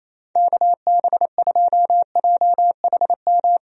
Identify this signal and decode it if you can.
Morse code, heard as a keyed tone, about 28 wpm, 700 Hz. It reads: KB2JHM
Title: USA Call Signs - 6 Characters - 28wpm